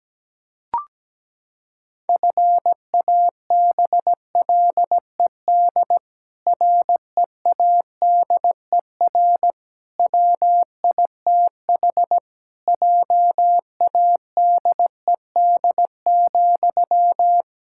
Morse code, heard as a keyed tone, about 17 wpm, 700 Hz. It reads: FABLED READER WITH JADED,
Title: Pangram Challenge v2 - 17wpm